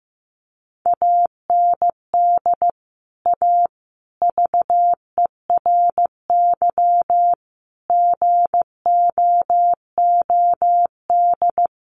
Morse code, heard as a keyed tone, about 15 wpm, 700 Hz. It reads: AND A VERY GOOD
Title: Sentences from Top 200 Words 15wpm